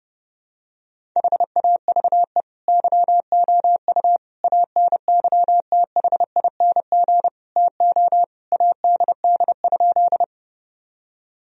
Morse code, heard as a keyed tone, about 30 wpm, 700 Hz. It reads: HAVE YOU ANYTHING TO ADD?
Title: Sentences from Top 1000 Words - Review Part 1 - 30wpm